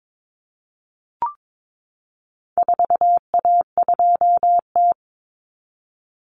Morse code, heard as a keyed tone, about 22 wpm, 700 Hz. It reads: 4A2T
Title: Common Cut Numbers - 4-Digit 22wpm